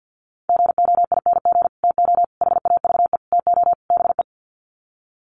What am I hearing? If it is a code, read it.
Morse code, heard as a keyed tone, about 50 wpm, 700 Hz. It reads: GOING TO HAVE TO BE
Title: Sets of 5 Words 50wpm